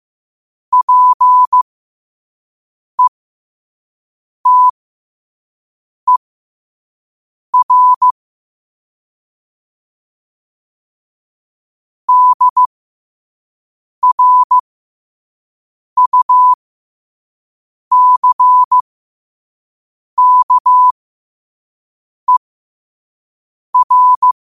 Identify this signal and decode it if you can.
Morse code, heard as a keyed tone, about 15 wpm, spaced as for 5 wpm, 1000 Hz. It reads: PETER DRUCKER
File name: qotdfiles / 5wpm